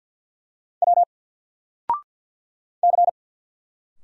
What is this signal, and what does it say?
Morse code, heard as a keyed tone, about 48 wpm, 700 Hz. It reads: W C
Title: Single Character (Letter:Number:Punctuation:Prosign) Alt 50wpm